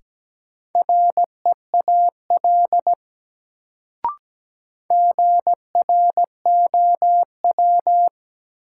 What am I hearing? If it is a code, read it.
Morse code, heard as a keyed tone, about 17 wpm, 700 Hz. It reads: REAL GROW
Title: Sentences from Top 300 Words - New Word Drill - 17wpm